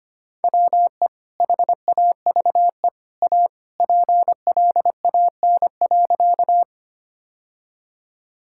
Morse code, heard as a keyed tone, about 25 wpm, 700 Hz. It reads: WE HAVE A PLAN.
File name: practice / 25wpm / Sentences with 2-Word Phrases - 25wpm